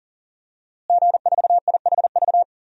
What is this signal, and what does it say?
Morse code, heard as a keyed tone, about 40 wpm, 700 Hz. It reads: G4IHV
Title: DXCC-100 Call Signs - 5 Characters - Repeated 2x - 40wpm